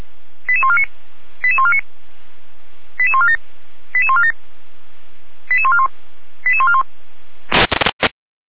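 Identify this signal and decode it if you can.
Five-tone selective call (ZVEI), heard as quick runs of five tones, tones of about 70 ms. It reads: 88149 88149 88147 88147 88141 88141
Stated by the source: ZVEI five-tone selective call received on SDR.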